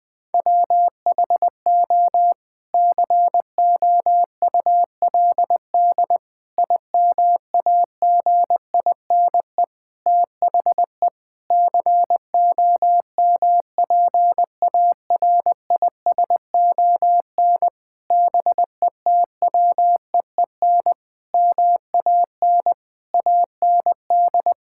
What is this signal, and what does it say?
Morse code, heard as a keyed tone, about 20 wpm, 700 Hz. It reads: WHO COULD IMAGINE THE COMPARISON BETWEEN MAN AND
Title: Sentences from Top 1600 Words 20wpm